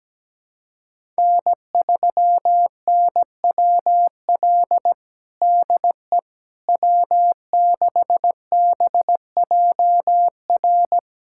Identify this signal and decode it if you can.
Morse code, heard as a keyed tone, about 17 wpm, 700 Hz. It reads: N3NWL DE W6BJR
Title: USA Callsign DE Callsign - 5 Characters - 17wpm